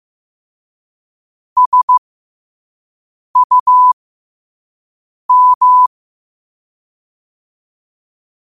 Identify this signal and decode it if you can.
Morse code, heard as a keyed tone, about 15 wpm, spaced as for 5 wpm, 1000 Hz. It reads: SUM